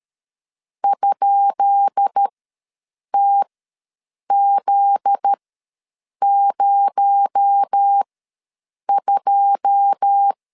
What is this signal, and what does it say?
Morse code, heard as a keyed tone, about 13 wpm, spaced as for 7 wpm, 800 Hz. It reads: ?TZ02
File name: LEZIONI / 10 SIGNS / 10B INT ws 12wpm 3527